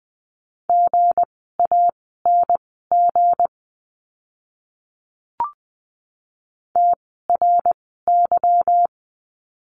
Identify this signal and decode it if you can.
Morse code, heard as a keyed tone, about 20 wpm, spaced as for 15 wpm, 700 Hz. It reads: GANG TRY
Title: 160 Lesson 26f - Letter g - Words 20wpm